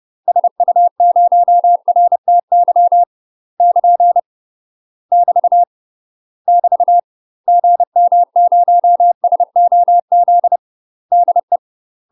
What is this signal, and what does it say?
Morse code, heard as a keyed tone, about 30 wpm, 700 Hz. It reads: SU0RTY <KN> <BT> <BT> GM0SOZ DE
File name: UK MASTER QSOs TRAINING / 30WPM / UK MASTER QSOs 0029